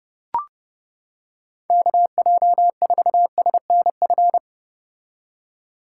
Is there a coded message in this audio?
Yes: Morse code from 1.7 to 4.4 s, heard as a keyed tone, about 31 wpm, 700 Hz. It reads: KJ4SNF